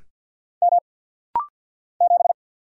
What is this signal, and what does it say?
Morse code, heard as a keyed tone, about 49 wpm, 700 Hz. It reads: M 7
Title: Single Letter-Number Alt - Rapid-Fire 50wpm